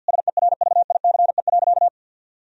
Morse code, heard as a keyed tone, about 51 wpm, 700 Hz. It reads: SERVICE.
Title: Sentences from Top 1000 Words - Review Part 4 - 50wpm